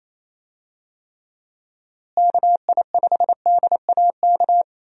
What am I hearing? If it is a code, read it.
Morse code, heard as a keyed tone, about 28 wpm, 700 Hz. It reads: KI5DAK
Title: USA Call Signs - 6 Characters - Repeated 3x - 28wpm